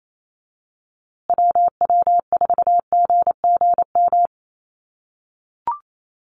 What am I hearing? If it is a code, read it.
Morse code, heard as a keyed tone, about 28 wpm, 700 Hz. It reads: WW4GGM
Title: USA Call Signs Repeated 2x 28wpm